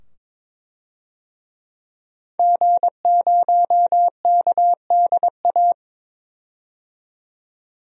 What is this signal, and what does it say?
Morse code, heard as a keyed tone, about 22 wpm, 700 Hz. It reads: G0KDA
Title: DXCC-100 Call Signs - 5 Characters - 22wpm